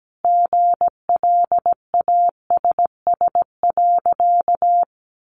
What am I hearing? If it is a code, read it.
Morse code, heard as a keyed tone, about 17 wpm, 700 Hz. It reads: GLASS.
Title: Sentences from Top 1300 Words 17wpm